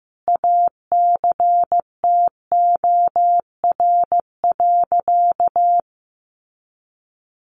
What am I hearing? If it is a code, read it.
Morse code, heard as a keyed tone, about 15 wpm, 700 Hz. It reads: ACTOR.
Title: Sentences from Top 1500 Words 15wpm